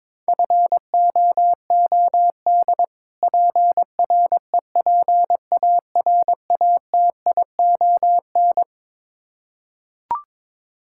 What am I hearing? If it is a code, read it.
Morse code, heard as a keyed tone, about 22 wpm, 700 Hz. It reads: FOOD PREPARATION
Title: Practice QSO - Exchange 3 - More Info - 22wpm